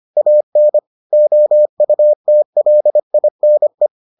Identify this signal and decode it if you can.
Morse code, heard as a keyed tone, about 25 wpm, 600 Hz. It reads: AN OUTLINE